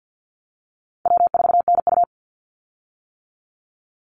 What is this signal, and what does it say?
Morse code, heard as a keyed tone, about 48 wpm, 700 Hz. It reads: W4NU
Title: Easy Call Signs - 50wpm